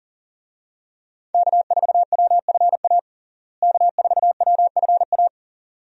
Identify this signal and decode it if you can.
Morse code, heard as a keyed tone, about 40 wpm, 700 Hz. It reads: K4WFA K4WFA
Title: USA Call Signs - 5 Characters - Repeated 2x - 40wpm